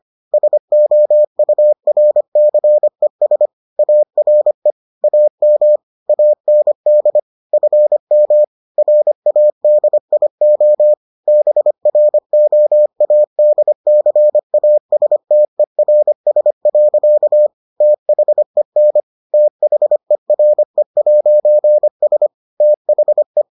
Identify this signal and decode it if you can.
Morse code, heard as a keyed tone, about 25 wpm, 600 Hz. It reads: SOURCES ARE AM AND FM RADIO BROADCASTERS. THEN THERE'S THE